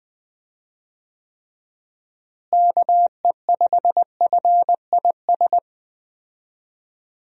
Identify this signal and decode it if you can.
Morse code, heard as a keyed tone, about 20 wpm, 700 Hz. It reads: KE5FIS
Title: USA Call Signs - 6 Characters - Repeated 2x - 20wpm